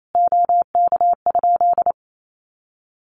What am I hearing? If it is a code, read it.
Morse code, heard as a keyed tone, about 28 wpm, 700 Hz. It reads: OK?
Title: Common QSO Phrases 28wpm